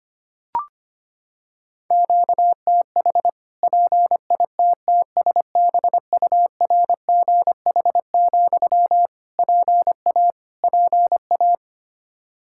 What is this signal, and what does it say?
Morse code, heard as a keyed tone, about 25 wpm, 700 Hz. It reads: QTH PITTSBURGH, PA PA
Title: Practice QSO - Exchange 1 - Beginning - 25wpm